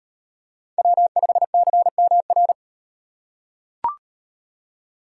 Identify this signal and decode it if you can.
Morse code, heard as a keyed tone, about 38 wpm, 700 Hz. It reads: W5CMR